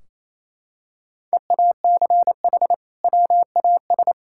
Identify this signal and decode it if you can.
Morse code, heard as a keyed tone, about 28 wpm, 700 Hz. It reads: EACH WAS
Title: Sentences from Top 1600 Words 28wpm